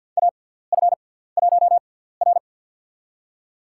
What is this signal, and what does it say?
Morse code, heard as a keyed tone, about 49 wpm, spaced as for 17 wpm, 700 Hz. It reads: AF1R
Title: USA Call Signs - 4 Characters - 6x Character Spacing - 50wpm